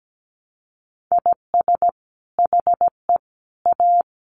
Morse code, heard as a keyed tone, about 17 wpm, 700 Hz. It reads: IS HE A
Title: Sentences from Top 1100 Words 17wpm